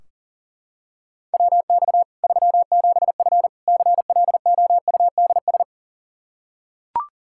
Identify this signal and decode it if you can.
Morse code, heard as a keyed tone, about 39 wpm, 700 Hz. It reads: WX 37F CLOUDS
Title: Practice QSO - Exchange 2 - Info - 40wpm